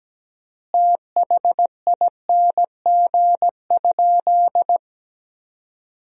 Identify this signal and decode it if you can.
Morse code, heard as a keyed tone, about 17 wpm, 700 Hz. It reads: THING?
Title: 3x Word-Spacing - Sentences from Top 200 Words - 17wpm